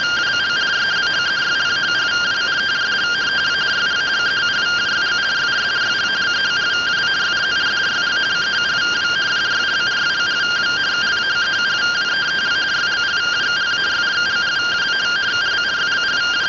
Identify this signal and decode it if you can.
Two-tone teleprinter signal, about 170 Hz shift.